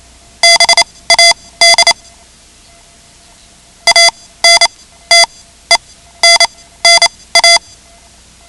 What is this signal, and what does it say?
Morse code, heard as a keyed tone, about 28 wpm, spaced as for 16 wpm, 2050 Hz. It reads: BAD ANTENNA
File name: ham radio / CodeQuick / Code Quick 10